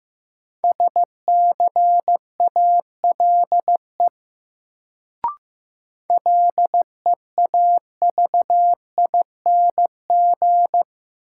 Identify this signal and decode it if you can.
Morse code, heard as a keyed tone, about 15 wpm, 700 Hz. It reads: SCALE LEAVING
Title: Sentences from Top 1000 Words - Review Part 4 - New Word Drill - 15wpm